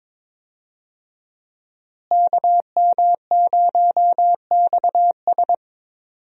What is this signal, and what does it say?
Morse code, heard as a keyed tone, about 22 wpm, 700 Hz. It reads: KM0XS
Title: USA Call Signs - 5 Characters - Repeated 2x - 22wpm